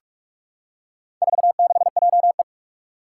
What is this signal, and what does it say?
Morse code, heard as a keyed tone, about 44 wpm, 700 Hz. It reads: 46JE
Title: Four Challenging Letter-Number Combinations 45wpm